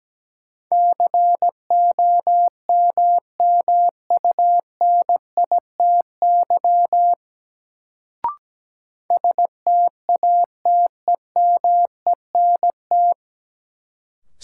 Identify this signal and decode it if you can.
Morse code, heard as a keyed tone, about 17 wpm, 700 Hz. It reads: COMMUNITY STATEMENT
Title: Sentences from Top 1200 Words - New Word Drill Alt - 17wpm